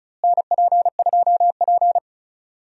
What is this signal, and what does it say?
Morse code, heard as a keyed tone, about 36 wpm, 700 Hz. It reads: NP2P
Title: USA Call Signs - 4 Characters - Repeated 3x - 35wpm